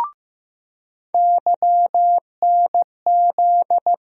Morse code, heard as a keyed tone, about 15 wpm, 700 Hz. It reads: YNZ
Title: USA Call Sign Postfixes - 3 Letters 15wpm